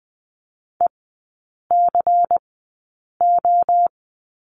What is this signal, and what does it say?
Morse code, heard as a keyed tone, about 20 wpm, spaced as for 8 wpm, 690 Hz. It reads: ECO
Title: 212 Lesson 34e - Number 8 - Three Characters 20wpm